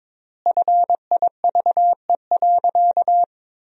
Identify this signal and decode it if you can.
Morse code, heard as a keyed tone, about 22 wpm, 700 Hz. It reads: FIVE.
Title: 4x Word-Spacing - Sentences from Top 400 Words - 22wpm